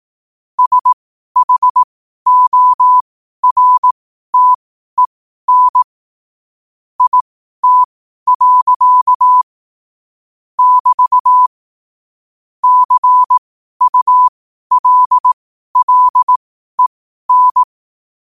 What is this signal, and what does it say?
Morse code, heard as a keyed tone, about 18 wpm, spaced as for 12 wpm, 1000 Hz. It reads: SHORTEN IT. <BT> CULLEN
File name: qotdfiles / 12wpm